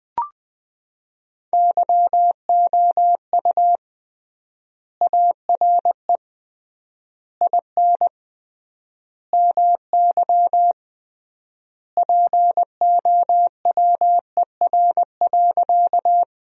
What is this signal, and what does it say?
Morse code, heard as a keyed tone, about 20 wpm, 700 Hz. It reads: YOU ARE IN MY POWER.
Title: 3x Word-Spacing - Sentences from Top 500 Words - 20wpm